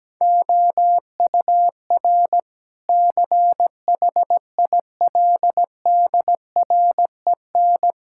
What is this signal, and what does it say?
Morse code, heard as a keyed tone, about 17 wpm, 700 Hz. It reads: OUR CHILDREN
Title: Sentences from Top 1000 Words - Review Part 5 - 17wpm